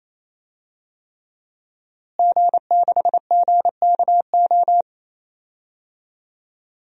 Morse code, heard as a keyed tone, about 28 wpm, 700 Hz. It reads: G6GKO
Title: DXCC-100 Call Signs - 5 Characters - Repeated 3x - 28wpm